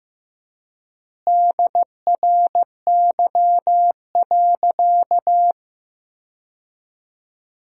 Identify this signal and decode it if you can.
Morse code, heard as a keyed tone, about 15 wpm, 700 Hz. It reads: DRY.
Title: 4x Word-Spacing - Sentences from Top 500 Words - 15wpm